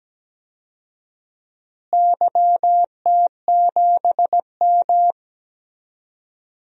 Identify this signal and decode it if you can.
Morse code, heard as a keyed tone, about 17 wpm, 700 Hz. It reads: YT7M
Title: DXCC-100 Call Signs - 4 Characters - Repeated 3x - 17wpm